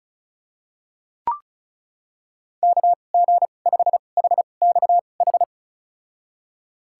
Morse code, heard as a keyed tone, about 36 wpm, spaced as for 25 wpm, 700 Hz. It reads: KG5HXH
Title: USA Call Signs - 6 Characters - 2x Character Spacing - 35wpm